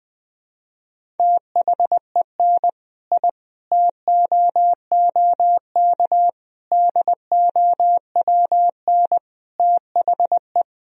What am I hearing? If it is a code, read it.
Morse code, heard as a keyed tone, about 20 wpm, 700 Hz. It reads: THEN I TOOK DOWN THE
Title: Sentences from Top 1000 Words - Review Part 1 - 20wpm